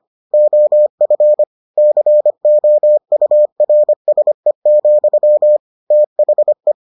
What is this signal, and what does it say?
Morse code, heard as a keyed tone, about 25 wpm, 600 Hz. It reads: OF COURSE, THE